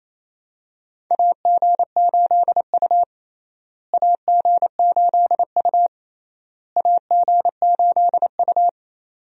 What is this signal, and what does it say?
Morse code, heard as a keyed tone, about 27 wpm, 700 Hz. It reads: AG8U AG8U AG8U
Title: USA Call Signs - 4 Characters - Repeated 3x - 28wpm